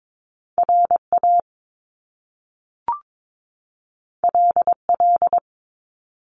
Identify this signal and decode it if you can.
Morse code, heard as a keyed tone, about 22 wpm, 700 Hz. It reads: RA LL
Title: Top Two Letter Combinations in Words 22wpm